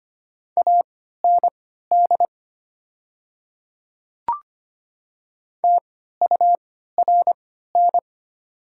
Morse code, heard as a keyed tone, about 25 wpm, spaced as for 14 wpm, 700 Hz. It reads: AND TURN
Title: 3x Character Spacing - Top 300 Words - 25wpm